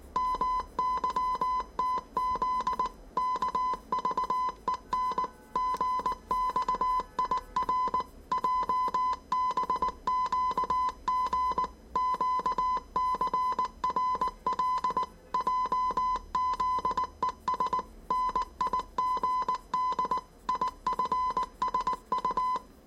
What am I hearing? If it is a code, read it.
Morse code, heard as a keyed tone, about 19 wpm, 1010 Hz. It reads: MYTZ KVEN GXIR JBQG QCRL JZES NIGD IFSU
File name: 业余无线电 / ●未分类 / 字码80